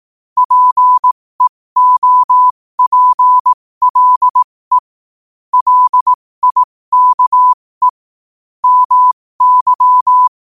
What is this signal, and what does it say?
Morse code, heard as a keyed tone, about 18 wpm, spaced as for 15 wpm, 1000 Hz. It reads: PEOPLE LIKE MY
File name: qotdfiles / 15wpm